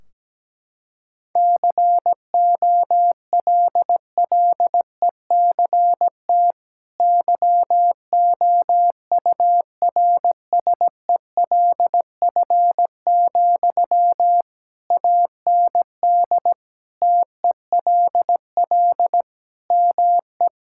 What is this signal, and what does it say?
Morse code, heard as a keyed tone, about 17 wpm, 700 Hz. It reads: COLLECT YOURSELF, AND TELL ME
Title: Sentences from Top 1600 Words 17wpm